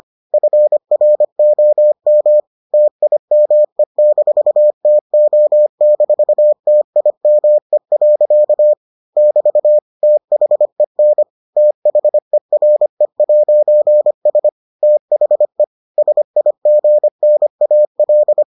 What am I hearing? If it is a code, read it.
Morse code, heard as a keyed tone, about 25 wpm, 600 Hz. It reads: FROM TIME-TO-TIME. <BT> THEN THERE'S THE SIGNAL